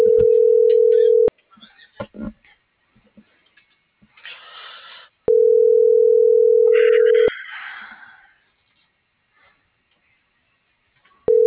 Ringback tone (North America), 440 + 480 Hz.